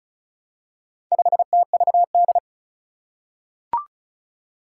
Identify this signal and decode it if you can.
Morse code, heard as a keyed tone, about 36 wpm, 700 Hz. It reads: 5TVD
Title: Four Challenging Letter-Number Combinations 35wpm